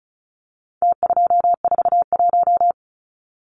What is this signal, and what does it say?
Morse code, heard as a keyed tone, about 34 wpm, 700 Hz. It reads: T241
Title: Numbers - Contest Style - 4-Digit 35wpm